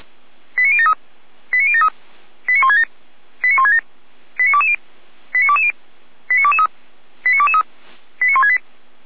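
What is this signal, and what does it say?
Five-tone selective call (ZVEI), heard as quick runs of five tones, tones of about 80 ms. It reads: 89073 89073 89167 89167 89220 89220 89223 89223 89168
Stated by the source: ZVEI five-tone fire brigade call 89123 received on an SDR, logged by Feuerwehr Orlamünde.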